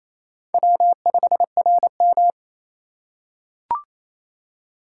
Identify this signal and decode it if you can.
Morse code, heard as a keyed tone, about 28 wpm, 700 Hz. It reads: W5RM
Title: USA Call Signs - 4 Characters - 3x Character Spacing - 28wpm